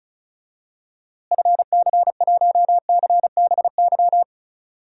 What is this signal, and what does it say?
Morse code, heard as a keyed tone, about 34 wpm, 700 Hz. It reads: FC1CBY
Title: DXCC-100 Call Signs - 6 Characters - Repeated 2x - 35wpm